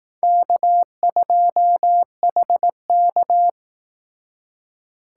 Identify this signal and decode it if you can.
Morse code, heard as a keyed tone, about 18 wpm, 700 Hz. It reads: K2HK
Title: Speed Racing - USA 1x2 Call Signs 15wpm